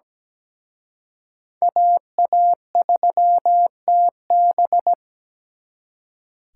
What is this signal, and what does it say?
Morse code, heard as a keyed tone, about 20 wpm, 700 Hz. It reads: AA3TB
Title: Speed Racing - USA 2x2 Call Signs 17wpm